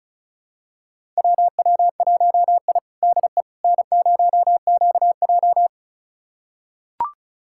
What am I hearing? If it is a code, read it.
Morse code, heard as a keyed tone, about 36 wpm, 700 Hz. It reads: WW1I DE N0QJ
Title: USA Callsign DE Callsign - 4 Characters - 35wpm